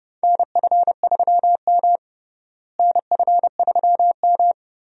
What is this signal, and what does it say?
Morse code, heard as a keyed tone, about 31 wpm, 700 Hz. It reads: NF3M NF3M
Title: USA Call Signs - 4 Characters - Repeated 2x - 30wpm